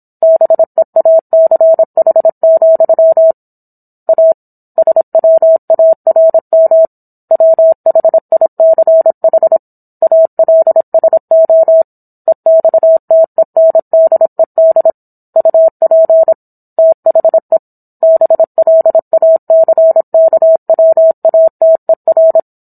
Morse code, heard as a keyed tone, about 27 wpm, 650 Hz. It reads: BEACH, A SWARM WHICH ALSO EXTENDED UP THE BLACKWATER
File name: Book One / 17-Chapter 17 - 26 WPM